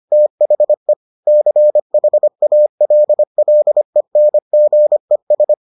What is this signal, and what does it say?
Morse code, heard as a keyed tone, about 25 wpm, 600 Hz. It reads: THE CHALLENGES